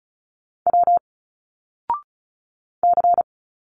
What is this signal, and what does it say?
Morse code, heard as a keyed tone, about 36 wpm, 700 Hz. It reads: W C
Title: Single Letters 35wpm